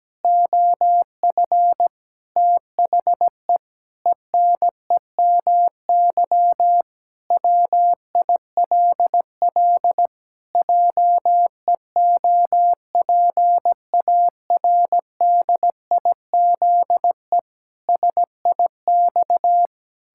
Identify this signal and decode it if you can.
Morse code, heard as a keyed tone, about 17 wpm, 700 Hz. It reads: OF THE ENEMY WILL JEOPARDIZE SIX
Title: Pangram Challenge v2 - 17wpm